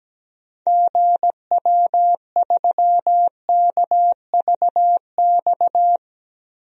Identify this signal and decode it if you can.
Morse code, heard as a keyed tone, about 17 wpm, 700 Hz. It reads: GW3KVX
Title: DXCC-100 Call Signs - 6 Characters - 17wpm